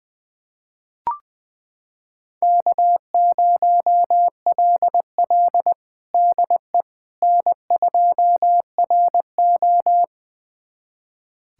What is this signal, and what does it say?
Morse code, heard as a keyed tone, about 20 wpm, 700 Hz. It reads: K0LL DE N2RO
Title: USA Callsign DE Callsign - 4 Characters - 20wpm